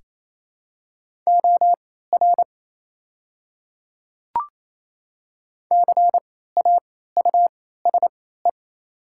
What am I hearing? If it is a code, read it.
Morse code, heard as a keyed tone, about 28 wpm, spaced as for 16 wpm, 700 Hz. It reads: OR CAUSE
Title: 3x Character Spacing - Top 300 Words - 28wpm